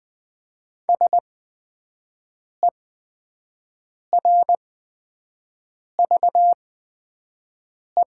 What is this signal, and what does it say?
Morse code, heard as a keyed tone, about 20 wpm, spaced as for 5 wpm, 700 Hz. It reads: SERVE
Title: 8x Character Spacing - Top 500 Words - 20wpm